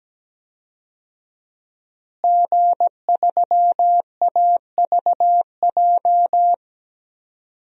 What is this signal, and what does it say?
Morse code, heard as a keyed tone, about 17 wpm, 700 Hz. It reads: G3AVJ
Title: DXCC-100 Call Signs - 5 Characters - Repeated 2x - 17wpm